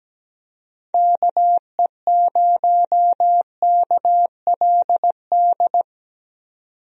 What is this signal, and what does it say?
Morse code, heard as a keyed tone, about 17 wpm, 700 Hz. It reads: KE0KLD
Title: USA Call Signs - 6 Characters - 17wpm